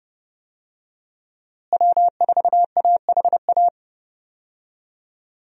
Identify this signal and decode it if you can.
Morse code, heard as a keyed tone, about 30 wpm, 700 Hz. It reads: W4AHA
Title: USA Call Signs - 5 Characters - Repeated 3x - 30wpm